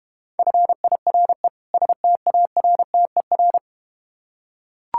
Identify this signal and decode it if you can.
Morse code, heard as a keyed tone, about 32 wpm, 700 Hz. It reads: FIRE STARTER